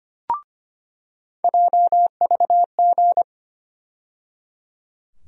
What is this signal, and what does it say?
Morse code, heard as a keyed tone, about 25 wpm, 700 Hz. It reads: JVG
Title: USA Call Sign Postfixes - 3 Letters 25wpm